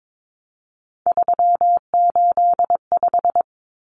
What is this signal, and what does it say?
Morse code, heard as a keyed tone, about 22 wpm, 700 Hz. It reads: 385
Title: Common Cut Numbers - 3-Digit 22wpm